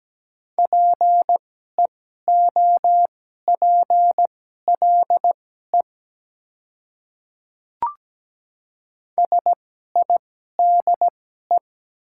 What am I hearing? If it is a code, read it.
Morse code, heard as a keyed tone, about 17 wpm, spaced as for 12 wpm, 700 Hz. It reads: PEOPLE SIDE
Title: 2x Character Spacing - Top 200 Words - 17wpm